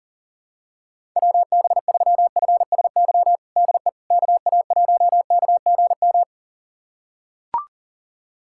Morse code, heard as a keyed tone, about 40 wpm, 700 Hz. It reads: WB3FSY DE KA1KGM
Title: USA Callsign DE Callsign - 6 Characters - 40wpm